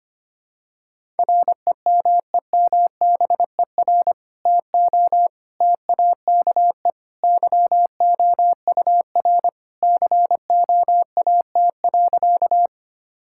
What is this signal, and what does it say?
Morse code, heard as a keyed tone, about 25 wpm, 700 Hz. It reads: REMEMBER TO TAKE YOUR COAT.
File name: practice / 25wpm / Sentences with 2-Word Phrases - 25wpm